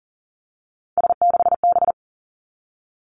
Morse code, heard as a keyed tone, about 42 wpm, 700 Hz. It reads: S6B
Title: Three Challenging Letter-Number Combinations 40wpm